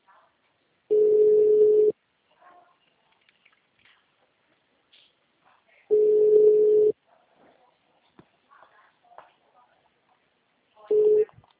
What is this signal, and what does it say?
Ringback tone (Europe, 425 Hz).